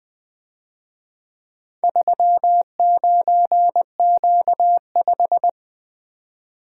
Morse code, heard as a keyed tone, about 22 wpm, 700 Hz. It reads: 39Q5
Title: Speed-Racing - Four Letter-Number Combinations - 20wpm